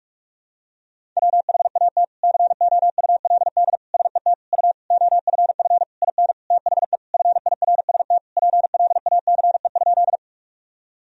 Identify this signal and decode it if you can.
Morse code, heard as a keyed tone, about 46 wpm, 700 Hz. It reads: WHAT COULD SET U OFF IN THE FIRST PLACE?
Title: Sentences from Top 200 Words and Abbreviations 45wpm